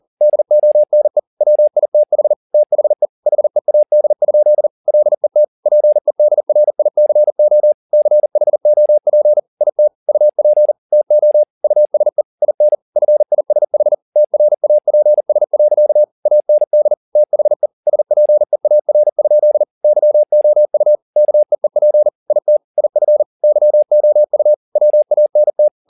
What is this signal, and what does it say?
Morse code, heard as a keyed tone, about 40 wpm, 600 Hz. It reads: DONE WITH THE HEAD? LET PEDRICO CHOP IT UP TO USE IN FISH TRAPS. AND THE SPEAR? YOU KEEP IT IF YOU WANT
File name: the old man and the sea 2 40wpm